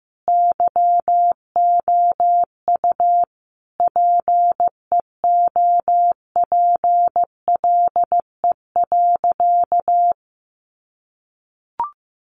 Morse code, heard as a keyed tone, about 15 wpm, 700 Hz. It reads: YOU PEOPLE.
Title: Sentences from Top 300 Words 15wpm